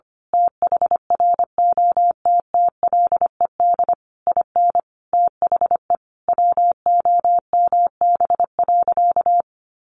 Morse code, heard as a keyed tone, about 25 wpm, 700 Hz. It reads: THROTTLED IN THE WOMB.
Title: A Sound Of Thunder 25wpm